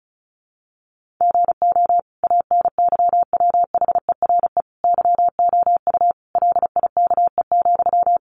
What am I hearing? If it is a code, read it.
Morse code, heard as a keyed tone, about 34 wpm, 700 Hz. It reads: GO ANYWHERE YOU LIKE,
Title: Sentences from Top 1400 Words 35wpm